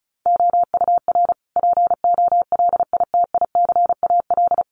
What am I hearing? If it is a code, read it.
Morse code, heard as a keyed tone, about 34 wpm, 700 Hz. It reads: OUR POLITICAL
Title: Sentences from Top 900 Words 35wpm